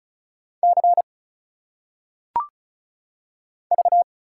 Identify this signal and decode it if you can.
Morse code, heard as a keyed tone, about 33 wpm, 700 Hz. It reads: C V
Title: Single Letters - Phonetic Pronunciation 35wpm